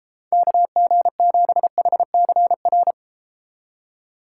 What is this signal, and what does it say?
Morse code, heard as a keyed tone, about 33 wpm, 700 Hz. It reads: KG7HCR